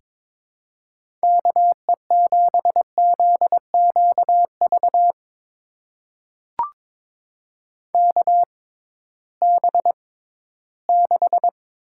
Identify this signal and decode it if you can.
Morse code, heard as a keyed tone, about 21 wpm, spaced as for 14 wpm, 700 Hz. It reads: KE7ZQV KB6
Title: USA Call Signs - 6 Characters - 6x Character Spacing - 22wpm